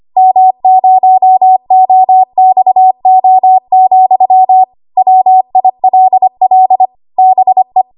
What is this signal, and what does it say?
Morse code, heard as a keyed tone, about 25 wpm, 750 Hz. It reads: M0OXO, WILL BE
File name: Archive / 25 WPM / 25WPM